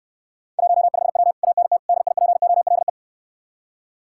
Morse code, heard as a keyed tone, about 69 wpm, 700 Hz. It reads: 15W ANT DIPOLE